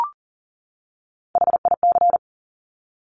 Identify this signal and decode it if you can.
Morse code, heard as a keyed tone, about 38 wpm, 700 Hz. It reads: HIC
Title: Top Three Letter Combinations in Words 40wpm